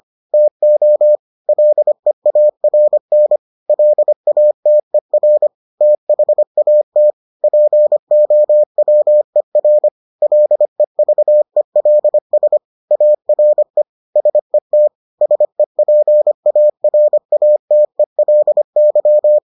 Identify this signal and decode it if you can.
Morse code, heard as a keyed tone, about 25 wpm, 600 Hz. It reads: TO LEARN LATER THAT POWER LEVELS ARE SET SEPARATELY